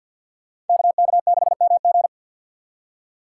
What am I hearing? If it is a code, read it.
Morse code, heard as a keyed tone, about 52 wpm, 700 Hz. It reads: KK6MG
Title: USA Call Signs - 5 Characters - 4x Character Spacing - 50wpm